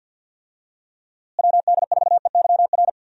Morse code, heard as a keyed tone, about 51 wpm, 700 Hz. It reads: WD4EYR
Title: USA Call Signs - 6 Characters - Repeated 3x - 50wpm